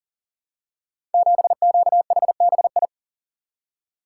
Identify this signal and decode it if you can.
Morse code, heard as a keyed tone, about 38 wpm, 700 Hz. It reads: 7QHBI
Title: Code Groups - 5 Letter:Numbers - 40wpm